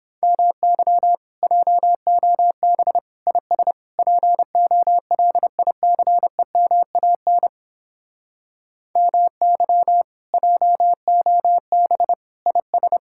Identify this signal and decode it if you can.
Morse code, heard as a keyed tone, about 29 wpm, 700 Hz. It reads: MY JOB IS POLICEMAN MY JOB IS